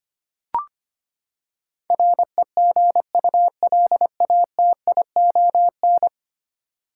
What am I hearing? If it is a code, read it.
Morse code, heard as a keyed tone, about 25 wpm, 700 Hz. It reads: REGULATION
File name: Ten Letter Words - 25wpm